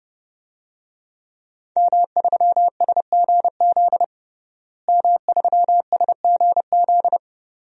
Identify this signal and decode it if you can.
Morse code, heard as a keyed tone, about 32 wpm, 700 Hz. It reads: M3SGZ M3SGZ
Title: DXCC-100 Call Signs - 5 Characters - Repeated 2x - 30wpm